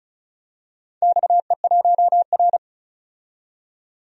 Morse code, heard as a keyed tone, about 36 wpm, 690 Hz. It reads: XE1R
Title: DXCC-100 Call Signs - 4 Characters - 35wpm